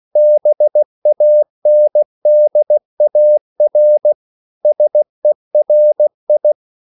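Morse code, heard as a keyed tone, about 16 wpm, 600 Hz. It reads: BANDAR SERI
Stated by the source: CW practice of national capitals at 16 words per minute, from ON6ZQ.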